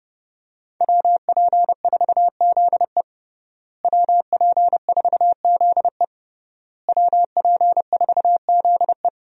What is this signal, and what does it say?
Morse code, heard as a keyed tone, about 32 wpm, 700 Hz. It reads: WP4ZE WP4ZE WP4ZE
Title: USA Call Signs - 5 Characters - Repeated 3x - 30wpm